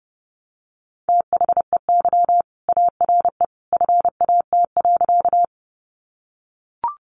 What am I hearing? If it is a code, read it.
Morse code, heard as a keyed tone, about 30 wpm, 700 Hz. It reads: THEY ARE FAT.
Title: Sentences from Top 600 Words 30wpm